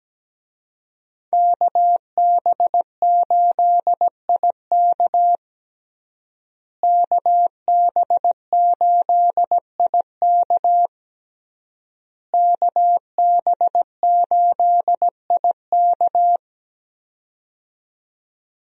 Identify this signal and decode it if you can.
Morse code, heard as a keyed tone, about 17 wpm, 700 Hz. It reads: KB8IK KB8IK KB8IK
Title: USA Call Signs - 5 Characters - Repeated 3x - 17wpm